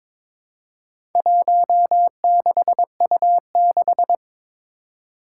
Morse code, heard as a keyed tone, about 22 wpm, 700 Hz. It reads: 16U6
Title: Four Challenging Letter-Number Combinations 22wpm